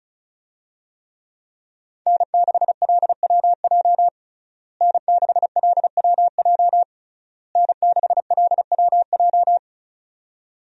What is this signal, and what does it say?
Morse code, heard as a keyed tone, about 36 wpm, 700 Hz. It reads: N6LWJ N6LWJ N6LWJ
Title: USA Call Signs - 5 Characters - Repeated 3x - 35wpm